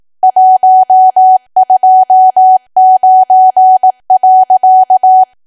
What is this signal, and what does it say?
Morse code, heard as a keyed tone, about 18 wpm, 750 Hz. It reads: 129.
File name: cwa / W1AW-Level2 / 18WPM